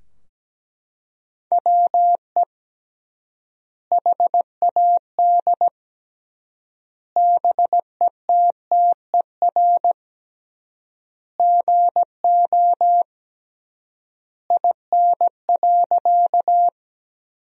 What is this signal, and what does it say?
Morse code, heard as a keyed tone, about 17 wpm, 700 Hz. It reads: WE HAD BETTER GO IN.
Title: 3x Word-Spacing - Sentences from Top 400 Words - 17wpm